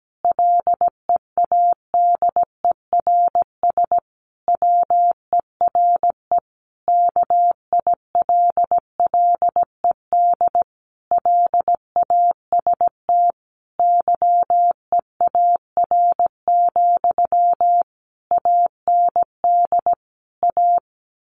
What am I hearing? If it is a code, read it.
Morse code, heard as a keyed tone, about 17 wpm, 700 Hz. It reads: LEADERS WERE KILLED LAST YEAR, AND A
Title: A Sound Of Thunder 17wpm